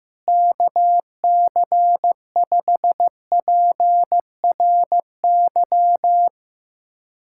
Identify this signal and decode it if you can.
Morse code, heard as a keyed tone, about 15 wpm, 700 Hz. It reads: KC5PRY
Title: USA Callsign DE Callsign - 6 Characters - 15wpm